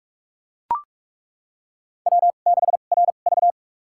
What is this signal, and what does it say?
Morse code, heard as a keyed tone, about 42 wpm, spaced as for 31 wpm, 690 Hz. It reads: W6RV
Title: USA Call Signs - 4 Characters - 2x Character Spacing - 45wpm